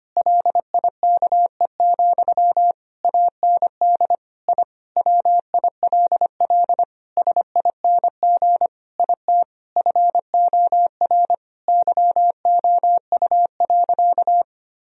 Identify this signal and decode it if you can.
Morse code, heard as a keyed tone, about 25 wpm, 700 Hz. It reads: LIKE, AND I WILL SING IT FOR YOU.
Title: Sentences from Top 1000 Words - Review Part 2 - 25wpm